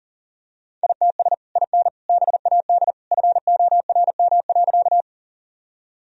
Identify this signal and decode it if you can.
Morse code, heard as a keyed tone, about 40 wpm, 700 Hz. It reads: ITS IN BAD FORM.
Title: Sentences from Top 1000 Words - Review Part 1 - 40wpm